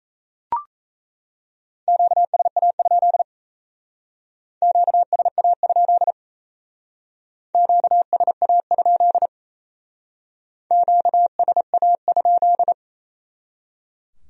Morse code, heard as a keyed tone, about 35 wpm, 700 Hz. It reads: QSA? QSA? QSA? QSA?